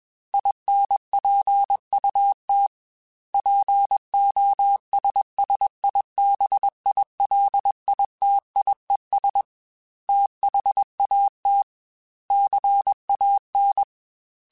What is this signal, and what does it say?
Morse code, heard as a keyed tone, about 21 wpm, 800 Hz. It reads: INPUT POSSIBILITIES THAT CAN
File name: Technical Subjects / Technical Subjects-mp3s / OR Gate @20